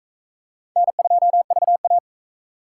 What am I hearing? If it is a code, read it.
Morse code, heard as a keyed tone, about 38 wpm, 700 Hz. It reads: N2VA